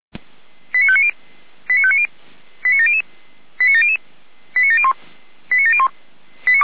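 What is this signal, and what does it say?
Five-tone selective call (ZVEI), heard as quick runs of five tones, tones of about 80 ms. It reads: 89590 89590 89700 89700 89701 89701
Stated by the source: ZVEI five-tone call received by SDR.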